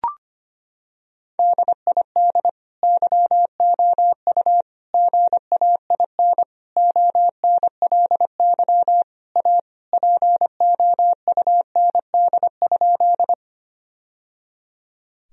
Morse code, heard as a keyed tone, about 25 wpm, 700 Hz. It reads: DID YOU GAIN ONLY A POUND?
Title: Sentences from Top 1000 Words - Review Part 2 - 25wpm